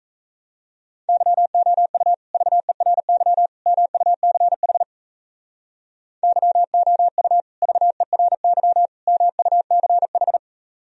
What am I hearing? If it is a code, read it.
Morse code, heard as a keyed tone, about 36 wpm, 700 Hz. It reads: YOU VERY MUCH YOU VERY MUCH